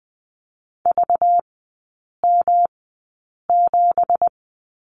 Morse code, heard as a keyed tone, about 20 wpm, spaced as for 8 wpm, 700 Hz. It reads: VM7
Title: 191 Lesson 31e - Letter v - Three Characters 20wpm